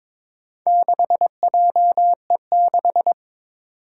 Morse code, heard as a keyed tone, about 22 wpm, 700 Hz. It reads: 6JE6
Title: Four Challenging Letter-Number Combinations 22wpm